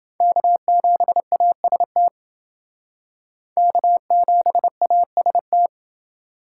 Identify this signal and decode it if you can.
Morse code, heard as a keyed tone, about 28 wpm, 700 Hz. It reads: K7AST K7AST